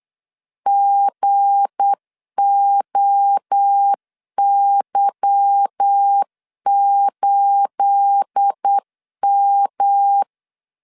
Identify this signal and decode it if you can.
Morse code, heard as a keyed tone, about 8 wpm, 800 Hz. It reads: GOY8M
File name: LEZIONI / 10 SIGNS / 10C DOT 8wpm 3649